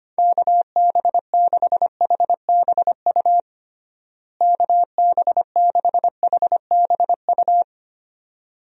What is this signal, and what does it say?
Morse code, heard as a keyed tone, about 25 wpm, 700 Hz. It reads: KB6HBU KB6HBU
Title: USA Call Signs - 6 Characters - Repeated 2x - 25wpm